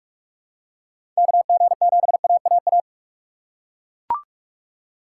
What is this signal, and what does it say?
Morse code, heard as a keyed tone, about 42 wpm, 700 Hz. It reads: KG7AAA
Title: USA Call Signs - 6 Characters - 3x Character Spacing - 45wpm